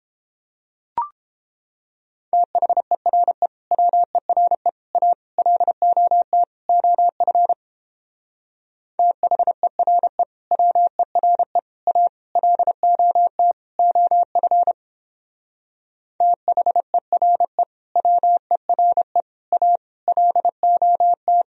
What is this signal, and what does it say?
Morse code, heard as a keyed tone, about 31 wpm, 700 Hz. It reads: THERE WERE A LOT OF THERE WERE A LOT OF THERE WERE A LOT